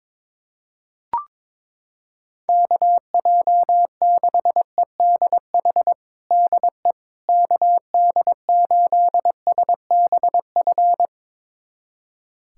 Morse code, heard as a keyed tone, about 22 wpm, 700 Hz. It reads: KJ6EDH DE KD8SBF
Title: USA Callsign DE Callsign - 6 Characters - 22wpm